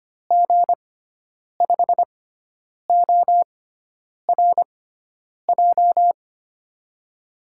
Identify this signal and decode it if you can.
Morse code, heard as a keyed tone, about 25 wpm, spaced as for 9 wpm, 700 Hz. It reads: G5ORJ
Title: USA Call Signs - 6 Characters - 6x Character Spacing - 25wpm